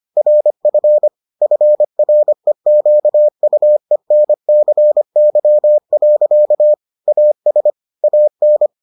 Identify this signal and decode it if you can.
Morse code, heard as a keyed tone, about 25 wpm, 600 Hz. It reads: RF FREQUENCY. AS AN